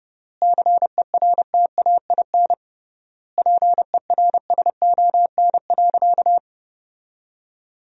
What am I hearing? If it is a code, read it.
Morse code, heard as a keyed tone, about 31 wpm, 700 Hz. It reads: CERTAIN PERSON.
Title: 3x Word-Spacing - Sentences from Top 500 Words - 30wpm